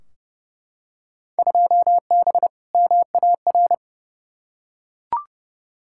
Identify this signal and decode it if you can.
Morse code, heard as a keyed tone, about 30 wpm, 700 Hz. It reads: 2B MAR
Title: Field Day Exchanges Encore - 30wpm